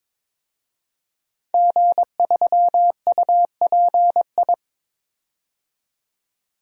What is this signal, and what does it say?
Morse code, heard as a keyed tone, about 22 wpm, 700 Hz. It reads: G3UPI
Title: DXCC-100 Call Signs - 5 Characters - 22wpm